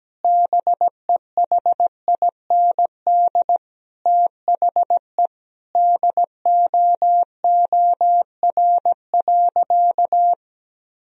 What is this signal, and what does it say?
Morse code, heard as a keyed tone, about 17 wpm, 700 Hz. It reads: BEHIND THE DOOR.
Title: Sentences from Top 1000 Words - Review Part 2 - 17wpm